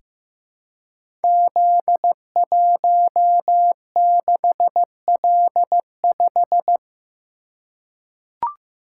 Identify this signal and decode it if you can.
Morse code, heard as a keyed tone, about 15 wpm, 700 Hz. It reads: Z16L5
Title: Code Groups - 5 Letter:Numbers - 15wpm